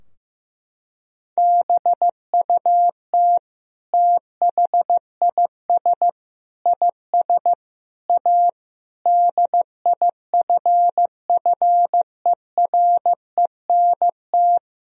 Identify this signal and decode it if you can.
Morse code, heard as a keyed tone, about 15 wpm, 700 Hz. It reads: BUT THIS IS A DIFFERENT
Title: Sentences from Top 1000 Words - Review Part 3 - 15wpm